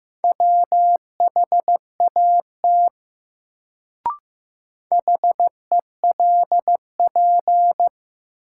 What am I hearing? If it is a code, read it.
Morse code, heard as a keyed tone, about 15 wpm, 700 Hz. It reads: WHAT HELP
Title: Sets of 2 Words - New Word Drill - 15wpm